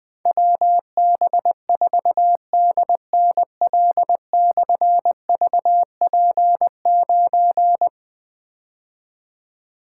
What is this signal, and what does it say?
Morse code, heard as a keyed tone, about 21 wpm, 700 Hz. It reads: WB4DNL/VP9
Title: Difficult Call Signs 20wpm